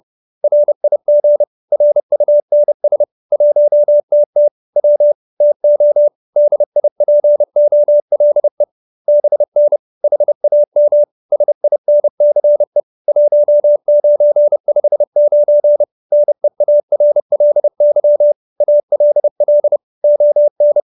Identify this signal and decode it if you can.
Morse code, heard as a keyed tone, about 30 wpm, 590 Hz. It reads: RIG RUNS 1TT W TO DIPOLE BN HAM SINCE 1959 NEARLY ALL ON
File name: QSO314 30